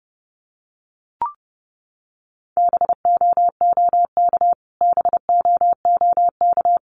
Morse code, heard as a keyed tone, about 30 wpm, 700 Hz. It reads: BOOK BOOK
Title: Top 300 Words Repeated 30wpm